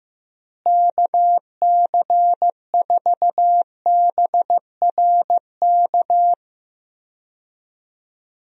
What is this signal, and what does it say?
Morse code, heard as a keyed tone, about 15 wpm, 700 Hz. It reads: KC4BRK
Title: USA Call Signs - 6 Characters - Repeated 2x - 15wpm